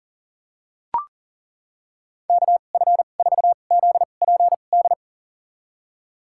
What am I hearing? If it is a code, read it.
Morse code, heard as a keyed tone, about 40 wpm, spaced as for 29 wpm, 700 Hz. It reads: KF4ZPD
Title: USA Call Signs - 6 Characters - 2x Character Spacing - 40wpm